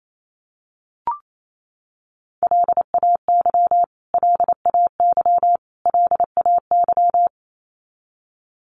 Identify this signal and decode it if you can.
Morse code, heard as a keyed tone, about 28 wpm, 700 Hz. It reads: LAY LAY LAY
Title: Top 500 Words Repeated 28wpm